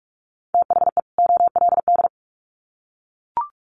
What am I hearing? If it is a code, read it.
Morse code, heard as a keyed tone, about 44 wpm, 700 Hz. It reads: THE OLD
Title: Sets of 2 Words 45wpm